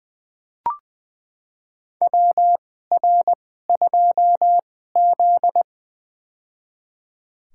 Morse code, heard as a keyed tone, about 20 wpm, spaced as for 14 wpm, 700 Hz. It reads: WR2Z
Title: USA Call Signs - 2x Character Spacing - 20wpm